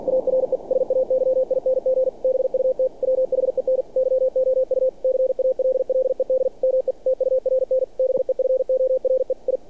Morse code, heard as a keyed tone, about 48 wpm, 520 Hz. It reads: ARE SAYING BUT WHEN YOU CALLED ME TURN BEFORE I
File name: archived / CRSnet